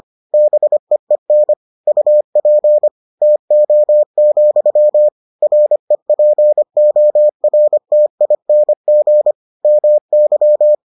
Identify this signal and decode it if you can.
Morse code, heard as a keyed tone, about 25 wpm, 600 Hz. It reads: BEEN UP TO, REPORTING MY